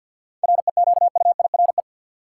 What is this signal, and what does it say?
Morse code, heard as a keyed tone, about 52 wpm, 700 Hz. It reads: REQUIRE
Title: Sentences from Top 900 Words - New Word Drill - 50wpm